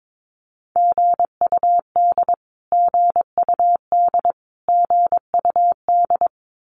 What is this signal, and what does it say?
Morse code, heard as a keyed tone, about 22 wpm, 700 Hz. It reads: GUD GUD GUD
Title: Abbreviations Repeated 3x - 22wpm